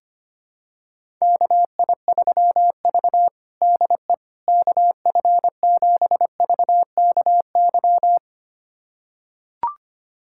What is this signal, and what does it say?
Morse code, heard as a keyed tone, about 25 wpm, 700 Hz. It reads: KI3V DE KF7VKY
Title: USA Callsign DE Callsign 25wpm